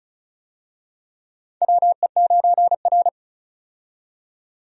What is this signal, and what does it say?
Morse code, heard as a keyed tone, about 34 wpm, 700 Hz. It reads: WE9R
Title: USA Call Signs - 4 Characters - 35wpm